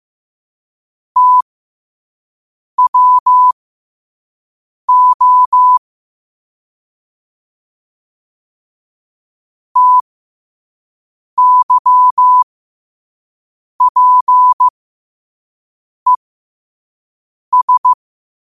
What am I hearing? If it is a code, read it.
Morse code, heard as a keyed tone, about 15 wpm, spaced as for 5 wpm, 1000 Hz. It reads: TWO TYPES
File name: qotdfiles / 5wpm